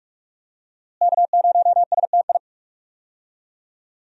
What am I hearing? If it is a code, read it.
Morse code, heard as a keyed tone, about 42 wpm, 700 Hz. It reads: K0STI
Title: USA Call Signs - 5 Characters - Repeated 2x - 45wpm